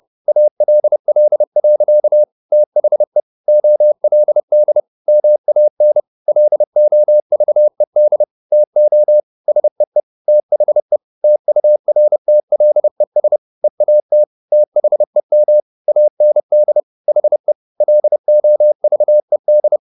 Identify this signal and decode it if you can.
Morse code, heard as a keyed tone, about 30 wpm, 600 Hz. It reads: ALL. THE OLD MAN LOVED TO SEE THE TURTLES EAT THEM AND HE LOVED
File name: the old man and the sea / the old man and the sea 1 30wpm